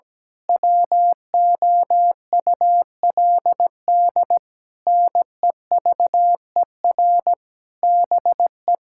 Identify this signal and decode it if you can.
Morse code, heard as a keyed tone, about 17 wpm, 700 Hz. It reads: WOULD NEVER BE
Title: Sentences from Top 1600 Words 17wpm